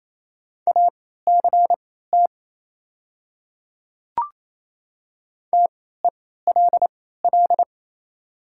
Morse code, heard as a keyed tone, about 28 wpm, spaced as for 16 wpm, 690 Hz. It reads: ACT TELL
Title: 3x Character Spacing - Top 200 Words - 28wpm